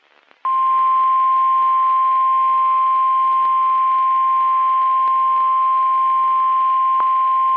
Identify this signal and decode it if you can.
Emergency Alert System (SAME) tones (the 1050 Hz attention tone). It is heard from 0.4 to 7.0 s.
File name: eas messages / EAS WXM31 Elmira